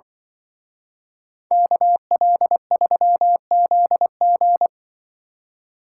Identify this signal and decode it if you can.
Morse code, heard as a keyed tone, about 24 wpm, 700 Hz. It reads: KL3ZG
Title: Speed Racing - USA 2x2 Call Signs 20wpm